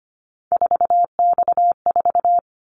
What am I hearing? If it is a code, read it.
Morse code, heard as a keyed tone, about 25 wpm, 700 Hz. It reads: 4X4
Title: Callsign Prefixes for Top 50 DXCC - 25wpm